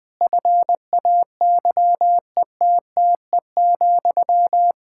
Morse code, heard as a keyed tone, about 20 wpm, 700 Hz. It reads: FAYETTE,
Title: Practice QSO - Exchange 1 - Beginning - 20wpm